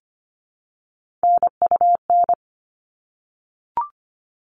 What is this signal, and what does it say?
Morse code, heard as a keyed tone, about 25 wpm, 700 Hz. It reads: NUN
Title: All Cut Numbers - 3-Digit 25wpm